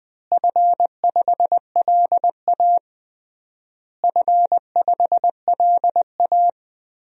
Morse code, heard as a keyed tone, about 21 wpm, 700 Hz. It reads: F5LA F5LA
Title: DXCC-100 Call Signs - 4 Characters - Repeated 3x - 20wpm